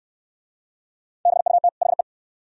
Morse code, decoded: BUT HE

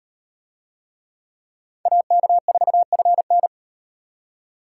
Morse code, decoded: AK4FN

